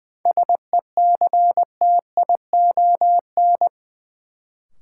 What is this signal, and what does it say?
Morse code, heard as a keyed tone, about 20 wpm, 700 Hz. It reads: SECTION
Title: Sentences from Top 1000 Words - Review Part 5 - New Word Drill - 20wpm